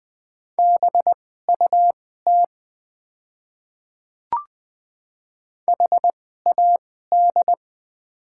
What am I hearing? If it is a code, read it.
Morse code, heard as a keyed tone, about 20 wpm, spaced as for 14 wpm, 700 Hz. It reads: BUT HAD
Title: 2x Character Spacing - Top 100 Words - 20wpm